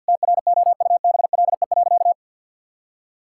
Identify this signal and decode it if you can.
Morse code, heard as a keyed tone, about 51 wpm, 700 Hz. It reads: TROUBLE.
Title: Sentences from Top 1600 Words 50wpm